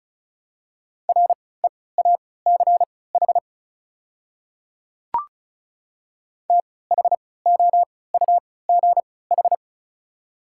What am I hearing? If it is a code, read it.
Morse code, heard as a keyed tone, about 34 wpm, spaced as for 20 wpm, 700 Hz. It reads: REACH THOUGH
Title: 3x Character Spacing - Top 500 Words - 35wpm